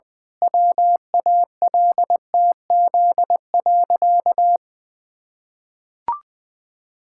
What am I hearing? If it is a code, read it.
Morse code, heard as a keyed tone, about 21 wpm, 700 Hz. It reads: WALTZ.
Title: Pangram Challenge v2 - 20wpm